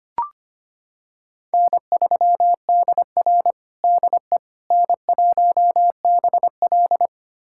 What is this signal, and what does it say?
Morse code, heard as a keyed tone, about 25 wpm, 700 Hz. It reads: N3DR DE N1BL
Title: USA Callsign DE Callsign - 4 Characters - 25wpm